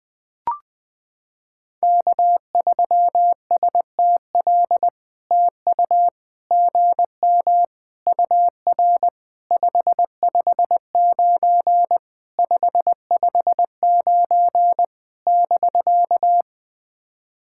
Morse code, heard as a keyed tone, about 20 wpm, 700 Hz. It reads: K3STL TU GM UR 559 559 <BK>
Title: POTA QSO Practice - 20wpm